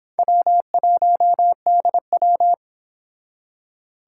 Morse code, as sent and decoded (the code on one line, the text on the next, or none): .-- .---- -.. .--
W1DW